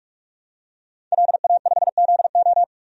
Morse code, heard as a keyed tone, about 44 wpm, 690 Hz. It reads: LA5ZO
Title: DXCC-100 Call Signs - 5 Characters - 45wpm